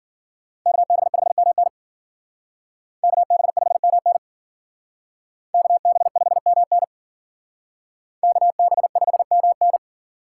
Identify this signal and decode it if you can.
Morse code, heard as a keyed tone, about 51 wpm, 700 Hz. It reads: KB5MN KB5MN KB5MN KB5MN